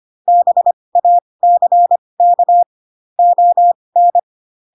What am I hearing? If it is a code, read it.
Morse code, heard as a keyed tone, about 25 wpm, spaced as for 20 wpm, 700 Hz. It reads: BACK ON
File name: Pinocchio / Pinocchio ch2 15 25 ik1hge